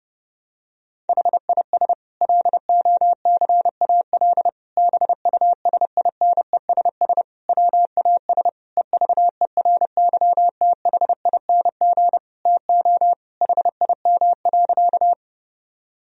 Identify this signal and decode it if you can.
Morse code, heard as a keyed tone, about 32 wpm, 700 Hz. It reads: HIS LOCAL BUSINESS WAS EVERYTHING TO HIM.
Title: Sentences from Top 1000 Words - Review Part 3 - 30wpm